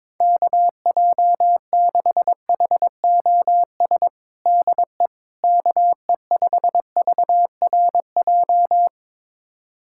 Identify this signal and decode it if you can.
Morse code, heard as a keyed tone, about 22 wpm, 700 Hz. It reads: KJ6HOS DE KE5VRJ
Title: USA Callsign DE Callsign - 6 Characters - 22wpm